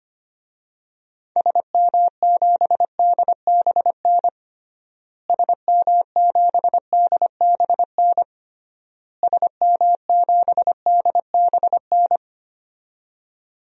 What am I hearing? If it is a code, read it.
Morse code, heard as a keyed tone, about 25 wpm, 700 Hz. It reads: SM7DBN SM7DBN SM7DBN
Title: DXCC-100 Call Signs - 6 Characters - Repeated 3x - 25wpm